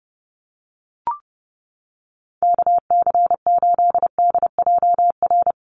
Morse code, heard as a keyed tone, about 30 wpm, 700 Hz. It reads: KC8DJR
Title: USA Call Signs Repeated 2x 30wpm